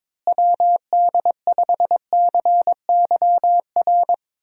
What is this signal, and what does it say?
Morse code, heard as a keyed tone, about 21 wpm, 700 Hz. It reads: WD5CYR
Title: USA Call Signs - 6 Characters - Repeated 3x - 22wpm